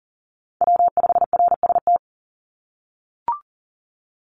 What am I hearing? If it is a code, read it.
Morse code, heard as a keyed tone, about 39 wpm, 700 Hz. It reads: W5RST
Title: NA SOTA Chaser Callsigns 40wpm